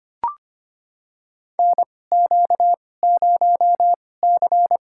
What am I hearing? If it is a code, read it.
Morse code, heard as a keyed tone, about 25 wpm, spaced as for 18 wpm, 700 Hz. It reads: NQ0C
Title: USA Call Signs - 4 Characters - 2x Character Spacing - 25wpm